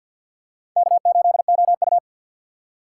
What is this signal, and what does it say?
Morse code, heard as a keyed tone, about 53 wpm, 700 Hz. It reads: K8OU
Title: USA Call Signs - 4 Characters - 3x Character Spacing - 50wpm